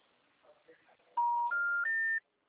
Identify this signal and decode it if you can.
Special information tone, with speech.
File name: out- -2017- - - .24037